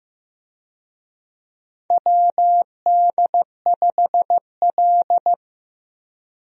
Morse code, heard as a keyed tone, about 15 wpm, 700 Hz. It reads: WD5L
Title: USA Call Signs - 4 Characters - Repeated 2x - 15wpm